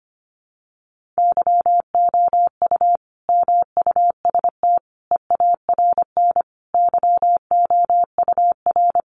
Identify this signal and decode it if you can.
Morse code, heard as a keyed tone, about 25 wpm, 700 Hz. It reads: YOU MUST EARN YOUR
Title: Sentences from Top 1500 Words 25wpm